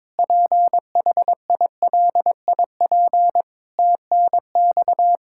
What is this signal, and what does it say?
Morse code, heard as a keyed tone, about 22 wpm, 700 Hz. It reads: PHILIP TNX
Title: Practice QSO - Exchange 3 - More Info - 22wpm